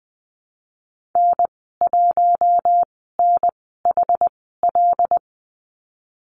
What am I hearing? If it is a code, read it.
Morse code, heard as a keyed tone, about 20 wpm, spaced as for 15 wpm, 700 Hz. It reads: N1NHL
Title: 128 Lesson 21g - Number 6 - Call Signs 20wpm